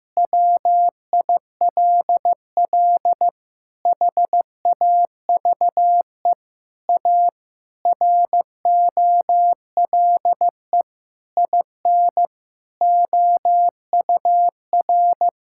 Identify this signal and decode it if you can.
Morse code, heard as a keyed tone, about 15 wpm, 700 Hz. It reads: WILL HAVE A ROLE IN OUR
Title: Sentences from Top 1000 Words - Review Part 6 - 15wpm